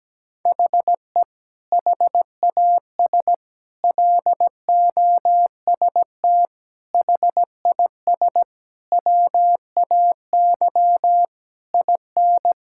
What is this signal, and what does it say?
Morse code, heard as a keyed tone, about 17 wpm, 700 Hz. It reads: HE HAS LOST HIS WAY IN
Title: Sentences from Top 1600 Words 17wpm